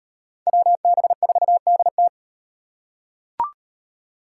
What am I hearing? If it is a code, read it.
Morse code, heard as a keyed tone, about 38 wpm, 700 Hz. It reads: WB4DT